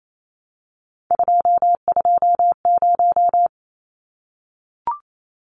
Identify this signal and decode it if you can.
Morse code, heard as a keyed tone, about 28 wpm, 700 Hz. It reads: 220
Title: Numbers - 3-Digit 28wpm